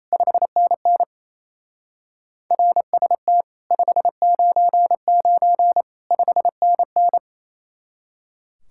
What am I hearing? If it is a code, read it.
Morse code, heard as a keyed tone, about 35 wpm, 700 Hz. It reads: 5NN RST 599 5NN
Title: Speed-Racing Common QSO Phrases - 28wpm